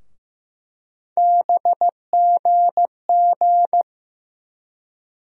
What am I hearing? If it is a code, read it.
Morse code, heard as a keyed tone, about 15 wpm, 700 Hz. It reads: BGG
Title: USA Call Sign Postfixes - 3 Letters 15wpm